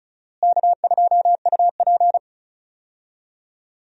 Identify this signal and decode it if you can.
Morse code, heard as a keyed tone, about 36 wpm, 700 Hz. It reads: K2UP
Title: USA Call Signs - 4 Characters - Repeated 3x - 35wpm